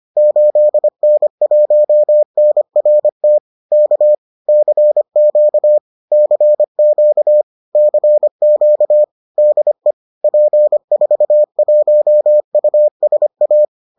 Morse code, heard as a keyed tone, about 25 wpm, 600 Hz. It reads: 8N1NRT K CQ CQ CQ DE P41USA